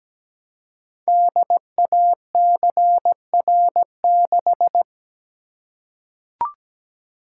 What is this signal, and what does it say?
Morse code, heard as a keyed tone, about 17 wpm, 700 Hz. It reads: DACR6
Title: Code Groups - 5 Letter:Numbers - 17wpm